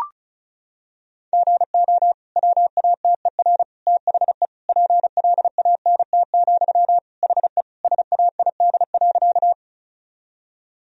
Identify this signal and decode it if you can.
Morse code, heard as a keyed tone, about 36 wpm, 700 Hz. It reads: GO WATER THE PLANT, HE SAID.
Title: Sentences from Top 1000 Words - Review Part 1 - 35wpm